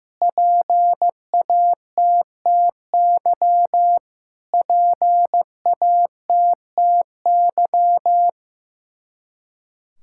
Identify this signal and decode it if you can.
Morse code, heard as a keyed tone, about 15 wpm, 700 Hz. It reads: PATTY PATTY
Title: Practice QSO - Exchange 1 - Beginning - 15wpm